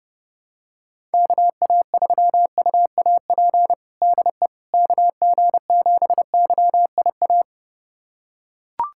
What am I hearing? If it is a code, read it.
Morse code, heard as a keyed tone, about 30 wpm, 700 Hz. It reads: KA3UAP DE KG7YIA
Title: USA Callsign DE Callsign - 6 Characters - 30wpm